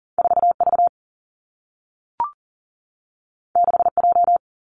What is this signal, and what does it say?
Morse code, heard as a keyed tone, about 39 wpm, 700 Hz. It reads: 4 V 6 J
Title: Two Challenging Letter-Number Combinations 40wpm